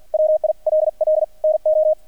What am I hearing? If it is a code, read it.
Morse code, heard as a keyed tone, about 50 wpm, spaced as for 42 wpm, 620 Hz. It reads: WILL NO
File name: archived / CRSnet